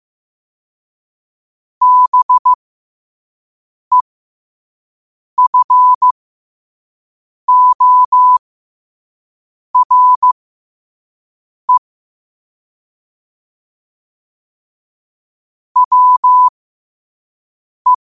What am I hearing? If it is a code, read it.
Morse code, heard as a keyed tone, about 15 wpm, spaced as for 5 wpm, 1000 Hz. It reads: BEFORE WE
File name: qotdfiles / 5wpm